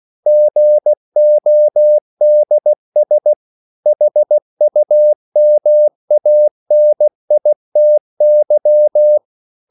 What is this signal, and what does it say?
Morse code, heard as a keyed tone, about 16 wpm, 600 Hz. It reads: GODS HUMANITY
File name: CW / Practice / EnglishQuotations0116